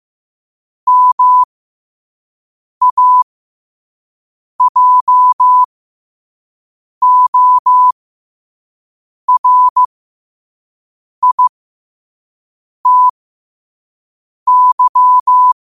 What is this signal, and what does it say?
Morse code, heard as a keyed tone, about 15 wpm, spaced as for 5 wpm, 1000 Hz. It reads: MAJORITY
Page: New quotes every day in morse code at 5 Words per minute.